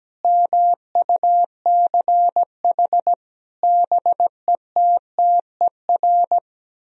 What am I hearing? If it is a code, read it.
Morse code, heard as a keyed tone, about 17 wpm, 700 Hz. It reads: MUCH BETTER
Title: Sentences from Top 1000 Words - Review Part 3 - 17wpm